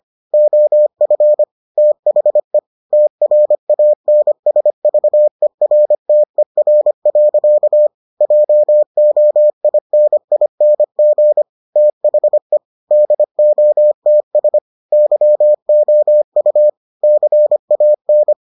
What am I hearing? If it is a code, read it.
Morse code, heard as a keyed tone, about 25 wpm, 600 Hz. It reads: OF THE TRANSVERTER. JOINING THE DOTS YOU CAN